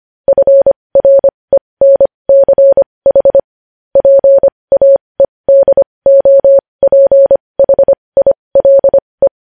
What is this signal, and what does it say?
Morse code, heard as a keyed tone, about 25 wpm, spaced as for 20 wpm, 550 Hz. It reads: FRENCH PAEDOPHILE
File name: podcast / News headlines-1x-25x20